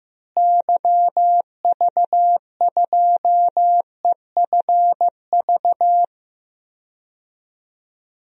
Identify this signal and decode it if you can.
Morse code, heard as a keyed tone, about 15 wpm, 700 Hz. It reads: YV2EFV
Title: DXCC-100 Call Signs - 6 Characters - Repeated 2x - 15wpm